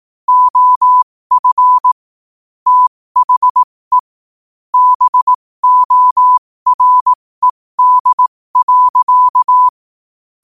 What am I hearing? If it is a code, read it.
Morse code, heard as a keyed tone, about 18 wpm, spaced as for 15 wpm, 1000 Hz. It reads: OF THE BORED.
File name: qotdfiles / 15wpm